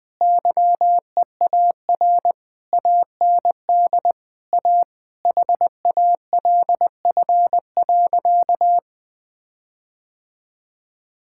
Morse code, heard as a keyed tone, about 21 wpm, 700 Hz. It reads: YEAR AND A HALF.
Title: Sentences with 5-Word Phrases - 20wpm